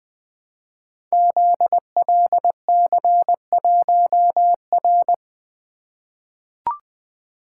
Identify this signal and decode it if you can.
Morse code, heard as a keyed tone, about 20 wpm, 700 Hz. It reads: ZLC1R
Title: Code Groups - 5 Letter:Numbers - 20wpm